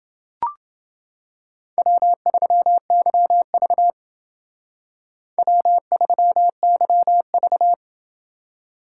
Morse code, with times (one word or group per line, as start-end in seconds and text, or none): 1.8-3.9 W3YV
5.4-7.7 W3YV